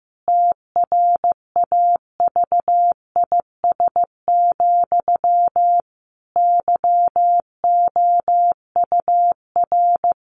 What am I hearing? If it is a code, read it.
Morse code, heard as a keyed tone, about 15 wpm, 700 Hz. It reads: TRAVIS, YOUR
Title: A Sound Of Thunder 15wpm